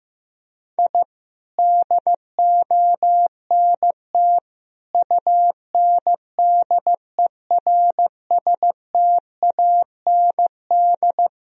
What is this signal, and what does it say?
Morse code, heard as a keyed tone, about 16 wpm, 700 Hz. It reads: I DONT UNDERSTAND
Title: Sentences from Top 1000 Words - Review Part 3 - 15wpm